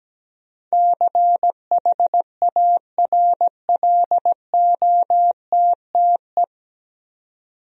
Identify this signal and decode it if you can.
Morse code, heard as a keyed tone, about 17 wpm, 700 Hz. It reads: CHARLOTTE
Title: USA Common Names - 17wpm